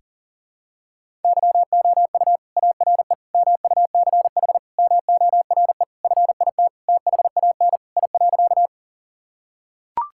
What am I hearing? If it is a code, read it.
Morse code, heard as a keyed tone, about 40 wpm, 690 Hz. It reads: YOU ARE MUCH MORE FIT THAN I.
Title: Sentences from Top 1000 Words - Review Part 5 - 40wpm